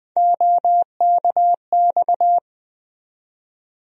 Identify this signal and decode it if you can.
Morse code, heard as a keyed tone, about 20 wpm, 700 Hz. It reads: OKX
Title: USA Call Sign Postfixes - 3 Letters 20wpm